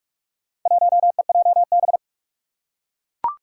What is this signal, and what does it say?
Morse code, heard as a keyed tone, about 44 wpm, 700 Hz. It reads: 1EJB
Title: Four Challenging Letter-Number Combinations 45wpm